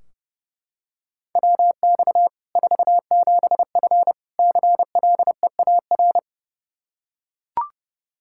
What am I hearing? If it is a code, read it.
Morse code, heard as a keyed tone, about 30 wpm, 700 Hz. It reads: WX 47F CLEAR
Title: Practice QSO - Exchange 2 - Info - 30wpm